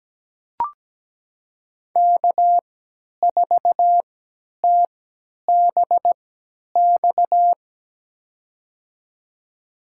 Morse code, heard as a keyed tone, about 17 wpm, spaced as for 10 wpm, 700 Hz. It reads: K4TBX
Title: USA Call Signs - 5 Characters - 3x Character Spacing - 17wpm